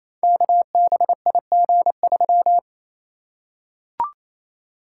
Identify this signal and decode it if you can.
Morse code, heard as a keyed tone, about 29 wpm, 700 Hz. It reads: KBIG3
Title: Code Groups - 5 Letter:Numbers - 28wpm